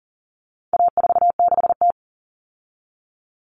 Morse code, heard as a keyed tone, about 39 wpm, 700 Hz. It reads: A46T
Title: Common Cut Numbers - 4-Digit 40wpm